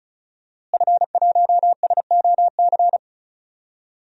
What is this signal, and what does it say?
Morse code, heard as a keyed tone, about 34 wpm, 700 Hz. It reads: F1SOC
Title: DXCC-100 Call Signs - 5 Characters - Repeated 2x - 35wpm